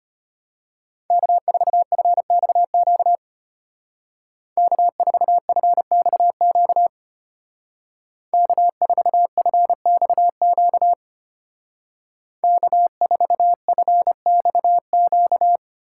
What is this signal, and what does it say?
Morse code, heard as a keyed tone, about 32 wpm, 700 Hz. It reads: K4FXQ K4FXQ K4FXQ K4FXQ